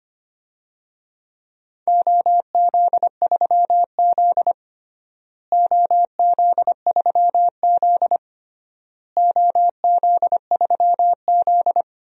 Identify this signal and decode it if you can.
Morse code, heard as a keyed tone, about 25 wpm, 700 Hz. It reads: OZ3Z OZ3Z OZ3Z
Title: DXCC-100 Call Signs - 4 Characters - Repeated 3x - 25wpm